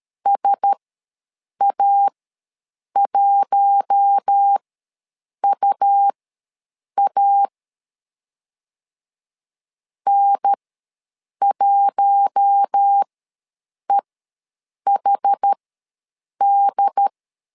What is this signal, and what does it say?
Morse code, heard as a keyed tone, about 13 wpm, spaced as for 7 wpm, 800 Hz. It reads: SA1UA N1EHD
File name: LEZIONI / 4 Q1S ws 12wpm 1827